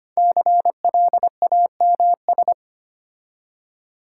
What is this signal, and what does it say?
Morse code, heard as a keyed tone, about 25 wpm, 700 Hz. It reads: CLAMS
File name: practice / 25wpm / Difficult Words with Hints - 25wpm